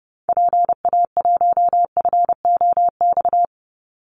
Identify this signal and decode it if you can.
Morse code, heard as a keyed tone, about 30 wpm, 700 Hz. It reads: PA1FOX
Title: Call Signs Repeated 2x - 30wpm